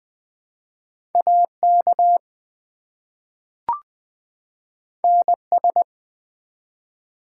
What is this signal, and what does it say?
Morse code, heard as a keyed tone, about 20 wpm, 700 Hz. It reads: AK NS
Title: ARRL Sections v2 - 20wpm